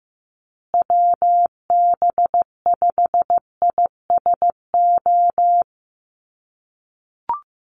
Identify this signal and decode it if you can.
Morse code, heard as a keyed tone, about 15 wpm, 700 Hz. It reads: WB5ISO
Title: USA Call Signs Repeated 3x 15wpm